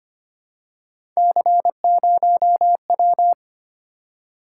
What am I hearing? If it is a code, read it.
Morse code, heard as a keyed tone, about 25 wpm, 700 Hz. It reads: C0W
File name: Three Letter-Number Combinations - 25wpm